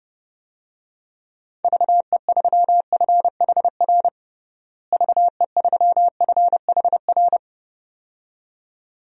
Morse code, heard as a keyed tone, about 30 wpm, 700 Hz. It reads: VE3FHR VE3FHR
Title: DXCC-100 Call Signs - 6 Characters - Repeated 2x - 30wpm